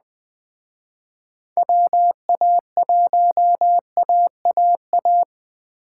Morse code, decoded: WA1AAA

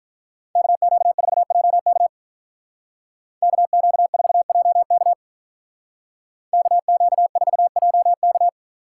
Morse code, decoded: KQ4JK KQ4JK KQ4JK

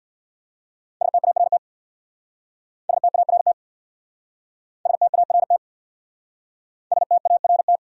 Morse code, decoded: START START START START

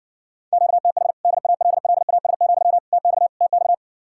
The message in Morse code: QTH DALLAS, TX TX

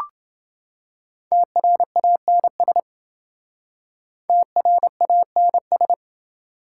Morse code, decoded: TRANS TRANS

